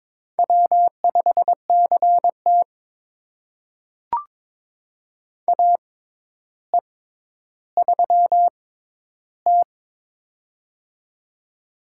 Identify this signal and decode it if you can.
Morse code, heard as a keyed tone, about 22 wpm, spaced as for 11 wpm, 700 Hz. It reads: W5CT AE3T
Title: USA Call Signs - 4 Characters - 6x Character Spacing - 22wpm